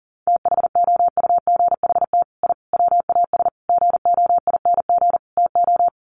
Morse code, heard as a keyed tone, about 40 wpm, 700 Hz. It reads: THOUGHT I WAS GOING TO
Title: Sets of 5 Words Encore2 40wpm